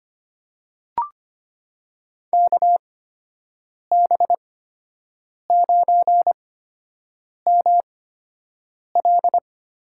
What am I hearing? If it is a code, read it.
Morse code, heard as a keyed tone, about 25 wpm, spaced as for 7 wpm, 700 Hz. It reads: KB9ML
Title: USA Call Signs - 6 Characters - 8x Character Spacing - 25wpm